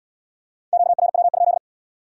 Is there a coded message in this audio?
Yes: Morse code from 0.7 to 1.6 s, heard as a keyed tone, about 70 wpm, 700 Hz. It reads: QRW?